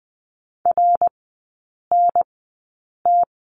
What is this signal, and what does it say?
Morse code, heard as a keyed tone, about 20 wpm, spaced as for 8 wpm, 700 Hz. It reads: RNT
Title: 053 Lesson 10e - Letter r - Three Characters 20wpm